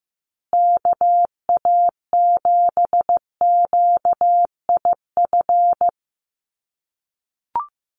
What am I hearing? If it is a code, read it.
Morse code, heard as a keyed tone, about 15 wpm, 700 Hz. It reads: KA7QIF
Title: USA Call Signs Repeated 2x 15wpm